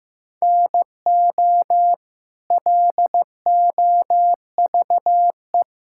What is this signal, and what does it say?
Morse code, heard as a keyed tone, about 15 wpm, 700 Hz. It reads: NO LOVE
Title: Sentences from Top 1600 Words 15wpm